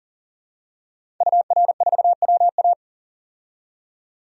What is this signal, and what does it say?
Morse code, heard as a keyed tone, about 40 wpm, 700 Hz. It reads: UR4WA
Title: DXCC-100 Call Signs - 5 Characters - 40wpm